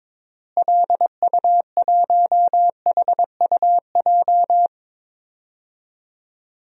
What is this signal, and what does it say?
Morse code, heard as a keyed tone, about 22 wpm, 700 Hz. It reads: LU1HUJ
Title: DXCC-100 Call Signs - 6 Characters - Repeated 3x - 22wpm